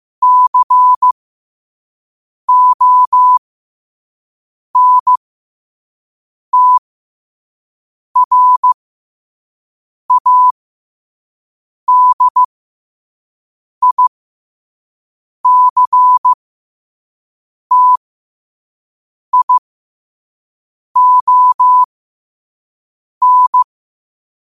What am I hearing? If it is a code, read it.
Morse code, heard as a keyed tone, about 15 wpm, spaced as for 5 wpm, 1000 Hz. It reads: CONTRADICTION
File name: qotdfiles / 5wpm